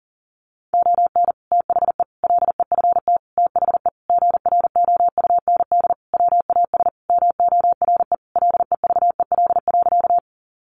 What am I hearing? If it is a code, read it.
Morse code, heard as a keyed tone, about 37 wpm, 700 Hz. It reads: ON THE LEFT THE GROUND WAS MORE LEVEL.
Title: Sentences from Top 700 Words 40wpm